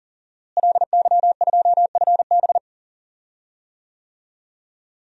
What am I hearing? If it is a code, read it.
Morse code, heard as a keyed tone, about 40 wpm, 700 Hz. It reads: LY2FB
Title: DXCC-100 Call Signs - 5 Characters - Repeated 2x - 40wpm